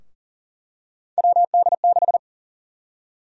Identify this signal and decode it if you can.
Morse code, heard as a keyed tone, about 40 wpm, 700 Hz. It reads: WD6
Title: USA Call Sign Prefixes - 40wpm